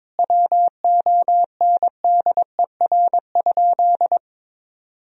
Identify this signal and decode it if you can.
Morse code, heard as a keyed tone, about 22 wpm, 700 Hz. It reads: WONDER?
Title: Sentences from Top 1000 Words - Review Part 3 - 22wpm